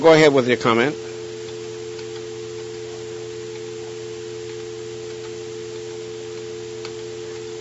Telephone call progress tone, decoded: dial tone